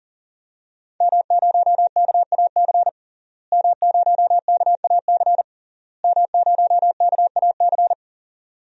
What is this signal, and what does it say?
Morse code, heard as a keyed tone, about 40 wpm, 700 Hz. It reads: M0KAC M0KAC M0KAC
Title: DXCC-100 Call Signs - 5 Characters - Repeated 3x - 40wpm